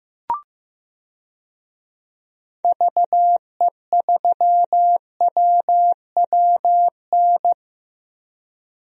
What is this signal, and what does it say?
Morse code, heard as a keyed tone, about 15 wpm, 700 Hz. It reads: VE3WWN
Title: DXCC-100 Call Signs - 6 Characters - 15wpm